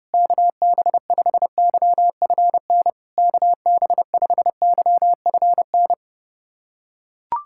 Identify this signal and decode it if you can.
Morse code, heard as a keyed tone, about 30 wpm, 690 Hz. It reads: KB5YFN KB5YFN
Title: SOTA QSO Practice 30wpm